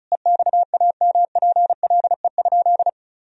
Morse code, heard as a keyed tone, about 34 wpm, 700 Hz. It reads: EXAMPLE?
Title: Sentences from Top 1000 Words - Review Part 1 - 35wpm